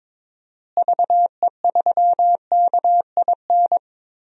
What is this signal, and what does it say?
Morse code, heard as a keyed tone, about 22 wpm, 700 Hz. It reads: VE3KIN
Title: DXCC-100 Call Signs - 6 Characters - Repeated 3x - 22wpm